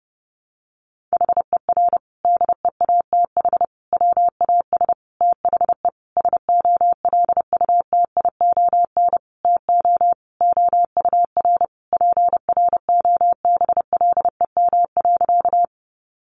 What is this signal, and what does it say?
Morse code, heard as a keyed tone, about 30 wpm, 700 Hz. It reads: HER DEATH WAS THE SOLUTION TO OUR PROBLEM.
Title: Sentences from Top 1300 Words 30wpm